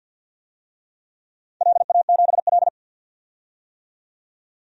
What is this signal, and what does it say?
Morse code, heard as a keyed tone, about 52 wpm, 700 Hz. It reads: LA7L
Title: DXCC-100 Call Signs - 4 Characters - 50wpm